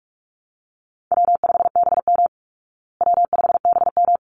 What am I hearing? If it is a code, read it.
Morse code, heard as a keyed tone, about 44 wpm, 700 Hz. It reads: W5BM W5BM
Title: Easy Call Signs Repeated 2x - 45wpm